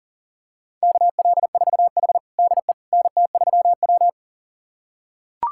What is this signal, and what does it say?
Morse code, heard as a keyed tone, about 37 wpm, 700 Hz. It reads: KL4H DE NT3W
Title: USA Callsign DE Callsign - 4 Characters - 40wpm